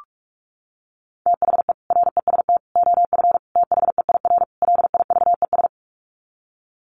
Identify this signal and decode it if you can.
Morse code, heard as a keyed tone, about 44 wpm, 700 Hz. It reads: THE REST OF THEIR LIVES
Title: Sets of 5 Words Encore 45wpm